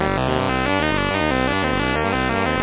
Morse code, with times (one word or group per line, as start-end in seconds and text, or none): none